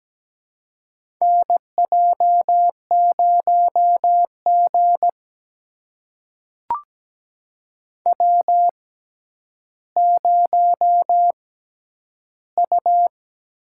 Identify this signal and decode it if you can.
Morse code, heard as a keyed tone, about 17 wpm, spaced as for 10 wpm, 700 Hz. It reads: NJ0G W0U
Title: USA Call Signs - 4 Characters - 6x Character Spacing - 17wpm